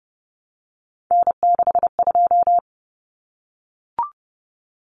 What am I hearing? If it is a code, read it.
Morse code, heard as a keyed tone, about 30 wpm, 700 Hz. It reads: N62
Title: Common Cut Numbers - 3-Digit 30wpm